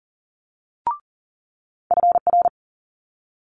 Morse code, heard as a keyed tone, about 40 wpm, 700 Hz. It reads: FR
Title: Multi-letter Phonemes Quick Fire 40wpm